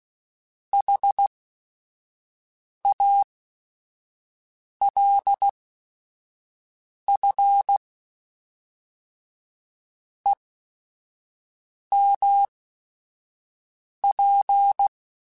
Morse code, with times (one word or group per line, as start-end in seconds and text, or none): 0.7-7.8 HALF
10.3-14.9 EMP